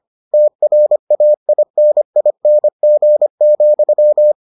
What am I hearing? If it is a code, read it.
Morse code, heard as a keyed tone, about 25 wpm, 600 Hz. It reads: TRAINING,